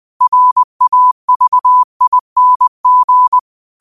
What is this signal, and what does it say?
Morse code, heard as a keyed tone, about 20 wpm, 1000 Hz. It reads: RAVING